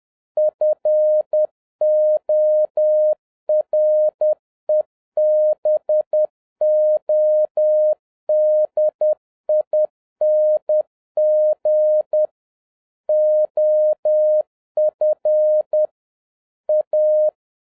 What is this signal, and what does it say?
Morse code, heard as a keyed tone, about 10 wpm, 610 Hz. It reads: FOREBODING OF A
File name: 2011-07-mp3 / n31 THE 3.11 2 10wpm